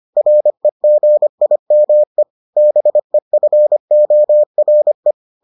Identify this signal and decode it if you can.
Morse code, heard as a keyed tone, about 25 wpm, 600 Hz. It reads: REGIME BEFORE